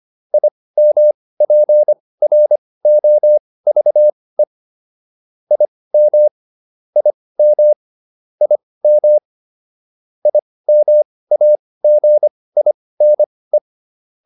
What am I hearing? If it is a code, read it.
Morse code, heard as a keyed tone, about 25 wpm, spaced as for 18 wpm, 600 Hz. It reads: IMPROVE IM IM IM IMAGINE